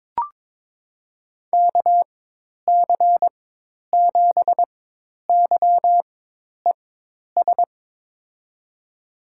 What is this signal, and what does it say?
Morse code, heard as a keyed tone, about 21 wpm, spaced as for 10 wpm, 700 Hz. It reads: KC7YES
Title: USA Call Signs - 4x Character Spacing - 22wpm